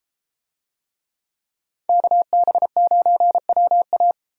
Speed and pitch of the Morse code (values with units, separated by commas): 33 wpm, 700 Hz